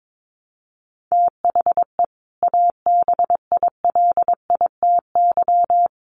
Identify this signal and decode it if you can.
Morse code, heard as a keyed tone, about 22 wpm, 700 Hz. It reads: THE ABILITY
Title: Sets of 3 Words 22wpm